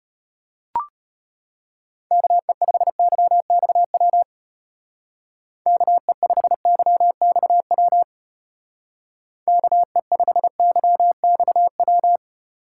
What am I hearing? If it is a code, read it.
Morse code, heard as a keyed tone, about 31 wpm, 700 Hz. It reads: KE5YXW KE5YXW KE5YXW
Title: Speed Racing - USA 2x3 Call Signs 25wpm